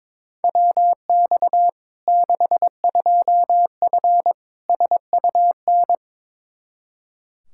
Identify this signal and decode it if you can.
Morse code, heard as a keyed tone, about 22 wpm, 700 Hz. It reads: WX 62F SUN
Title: Practice QSO - Exchange 2 - Info - 22wpm